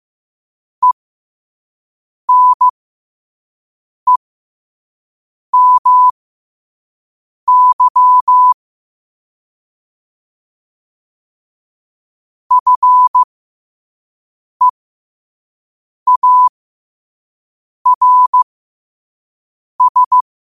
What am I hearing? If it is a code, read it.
Morse code, heard as a keyed tone, about 15 wpm, spaced as for 5 wpm, 1000 Hz. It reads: ENEMY FEARS
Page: New quotes every day in morse code at 5 Words per minute.